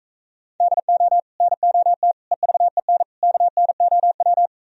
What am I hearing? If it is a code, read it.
Morse code, heard as a keyed tone, about 45 wpm, 700 Hz. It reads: DO NOT EVEN KNOW